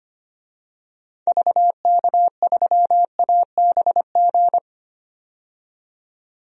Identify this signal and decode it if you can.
Morse code, heard as a keyed tone, about 25 wpm, 700 Hz. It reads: VK3ABG
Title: DXCC-100 Call Signs - 6 Characters - 25wpm